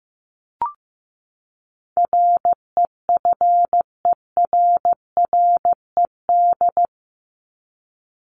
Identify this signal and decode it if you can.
Morse code, heard as a keyed tone, about 15 wpm, 700 Hz. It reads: REFERRED
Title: ED Suffix - Top 100 Words with Base Words 15wpm